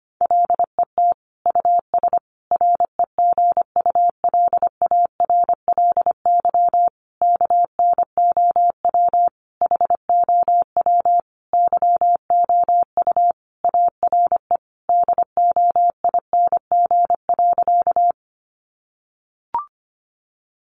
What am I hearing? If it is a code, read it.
Morse code, heard as a keyed tone, about 25 wpm, 700 Hz. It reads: LET US REGULARLY KNOW HOW YOU ARE DOING.
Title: Sentences from Top 1400 Words 25wpm